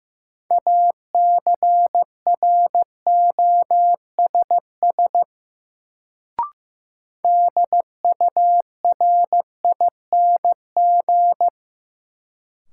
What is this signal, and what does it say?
Morse code, heard as a keyed tone, about 15 wpm, 700 Hz. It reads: ACROSS DURING
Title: Sentences from Top 700 Words - New Word Drill - 15wpm